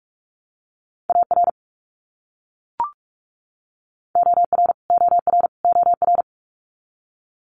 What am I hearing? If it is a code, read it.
Morse code, heard as a keyed tone, about 44 wpm, 700 Hz. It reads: AR OR OR OR
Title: US States Repeated 3x 45wpm